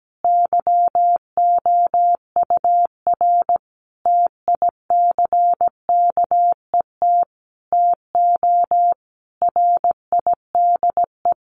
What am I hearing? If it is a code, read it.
Morse code, heard as a keyed tone, about 17 wpm, 700 Hz. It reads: YOUR TICKET TO RIDE
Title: Sentences from Top 1400 Words 17wpm